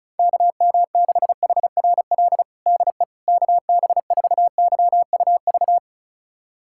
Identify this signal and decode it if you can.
Morse code, heard as a keyed tone, about 34 wpm, 700 Hz. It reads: KM6HRL DE KB4YUV
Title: USA Callsign DE Callsign 35wpm